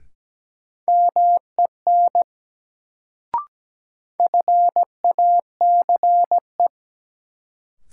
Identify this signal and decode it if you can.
Morse code, heard as a keyed tone, about 17 wpm, 700 Hz. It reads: MEN FACE
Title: Sets of 3 Words - New Word Drill - 17wpm